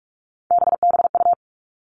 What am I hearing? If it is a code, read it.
Morse code, heard as a keyed tone, about 46 wpm, 700 Hz. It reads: BBU
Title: Three Challenging Letter-Number Combinations 45wpm